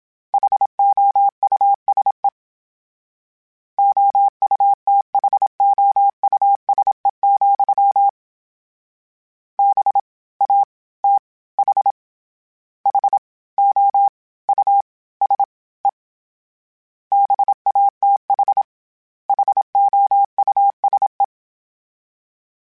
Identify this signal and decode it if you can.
Morse code, heard as a keyed tone, about 27 wpm, 800 Hz. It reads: HOUSE OUTHOUSE, B A T H H O U S E BATH HOUSE
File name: General / General-mp3s / Houses @25